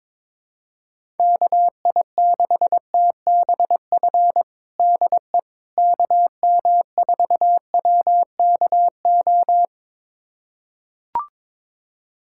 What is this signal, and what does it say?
Morse code, heard as a keyed tone, about 22 wpm, 700 Hz. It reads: KI6TBF DE KM4WKO
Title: USA Callsign DE Callsign 22wpm